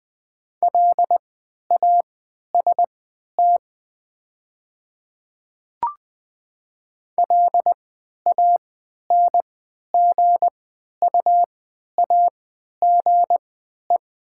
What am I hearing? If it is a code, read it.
Morse code, heard as a keyed tone, about 20 wpm, spaced as for 11 wpm, 700 Hz. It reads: LAST LANGUAGE
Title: 3x Character Spacing - Top 500 Words - 20wpm